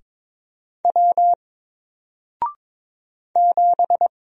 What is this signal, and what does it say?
Morse code, heard as a keyed tone, about 22 wpm, 700 Hz. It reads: W 7
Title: Single Character (Letter:Number:Punctuation:Prosign) Alt 22wpm